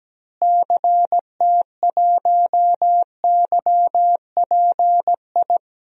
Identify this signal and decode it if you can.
Morse code, heard as a keyed tone, about 17 wpm, 700 Hz. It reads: CT1YPI
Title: DXCC-100 Call Signs - 6 Characters - Repeated 3x - 17wpm